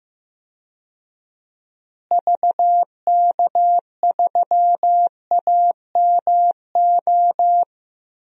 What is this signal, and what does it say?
Morse code, heard as a keyed tone, about 16 wpm, 700 Hz. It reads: VK3AMO
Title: DXCC-100 Call Signs - 6 Characters - Repeated 3x - 15wpm